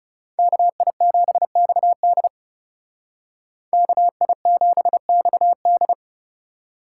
Morse code, heard as a keyed tone, about 38 wpm, 700 Hz. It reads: KI7XD KI7XD